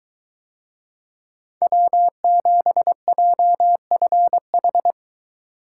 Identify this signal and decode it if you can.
Morse code, heard as a keyed tone, about 22 wpm, 700 Hz. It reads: W7JFH